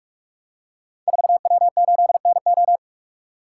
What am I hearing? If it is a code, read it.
Morse code, heard as a keyed tone, about 44 wpm, 700 Hz. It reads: 4W8NO
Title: Code Groups - 5 Letter:Numbers - 45wpm